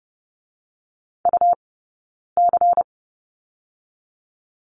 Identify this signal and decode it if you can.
Morse code, heard as a keyed tone, about 30 wpm, 700 Hz. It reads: U C
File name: Lesson 20d - Number 3 - Two Characters 30wpm